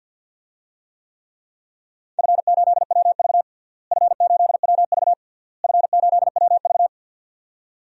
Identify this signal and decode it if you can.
Morse code, heard as a keyed tone, about 46 wpm, 700 Hz. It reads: F8WV F8WV F8WV
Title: DXCC-100 Call Signs - 4 Characters - Repeated 3x - 50wpm